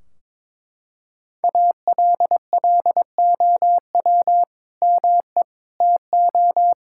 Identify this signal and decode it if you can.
Morse code, heard as a keyed tone, about 22 wpm, 700 Hz. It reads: ALLOW ME TO
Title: Sentences from Top 1600 Words 22wpm